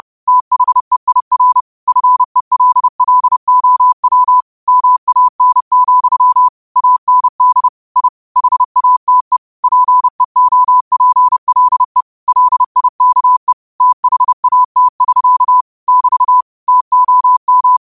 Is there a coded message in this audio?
Yes: Morse code from 0.3 to 17.8 s, heard as a keyed tone, about 30 wpm, 1000 Hz. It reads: THEIR FELLOW MAN, AND I HATE PEOPLE LIKE THAT <SK> <BT> TOM